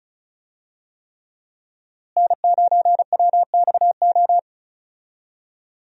Morse code, heard as a keyed tone, about 36 wpm, 700 Hz. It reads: N9WXO
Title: USA Call Signs - 5 Characters - Repeated 3x - 35wpm